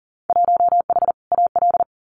Morse code, heard as a keyed tone, about 40 wpm, 700 Hz. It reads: 1H AL
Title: Winter Field Day Exchanges 40wpm